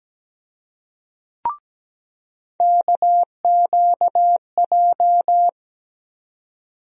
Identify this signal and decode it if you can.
Morse code, heard as a keyed tone, about 17 wpm, 700 Hz. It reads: KQJ
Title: USA Call Sign Postfixes - 3 Letters 17wpm